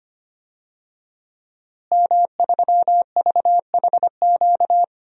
Morse code, heard as a keyed tone, about 25 wpm, 700 Hz. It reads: M3VHQ
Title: DXCC-100 Call Signs - 5 Characters - Repeated 2x - 25wpm